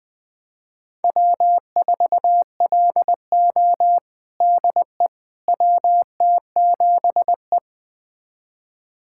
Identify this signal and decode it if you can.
Morse code, heard as a keyed tone, about 20 wpm, 700 Hz. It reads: W4LO DE WT7E
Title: USA Callsign DE Callsign - 4 Characters - 20wpm